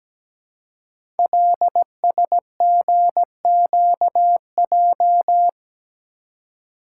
Morse code, heard as a keyed tone, about 17 wpm, 700 Hz. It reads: LSGQJ
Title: Code Groups - 5 Letter:Numbers - 17wpm